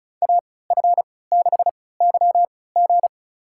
Morse code, heard as a keyed tone, about 36 wpm, spaced as for 20 wpm, 700 Hz. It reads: AF6YG
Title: USA Call Signs - 5 Characters - 3x Character Spacing - 35wpm